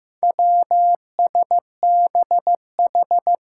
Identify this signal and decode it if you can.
Morse code, heard as a keyed tone, about 15 wpm, 700 Hz. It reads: WSBH
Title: Four Challenging Letter-Number Combinations 15wpm